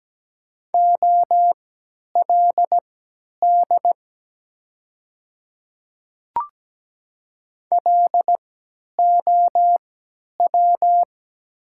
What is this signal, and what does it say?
Morse code, heard as a keyed tone, about 17 wpm, spaced as for 10 wpm, 700 Hz. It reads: OLD LOW
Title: 3x Character Spacing - Top 300 Words - 17wpm